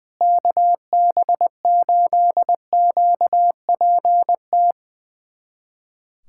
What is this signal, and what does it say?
Morse code, heard as a keyed tone, about 21 wpm, 700 Hz. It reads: KB8QPT
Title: USA Callsign DE Callsign - 6 Characters - 20wpm